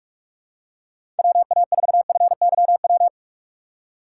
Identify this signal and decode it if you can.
Morse code, heard as a keyed tone, about 46 wpm, 700 Hz. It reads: WA4FYW
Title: USA Call Signs - 3x Character Spacing - 45wpm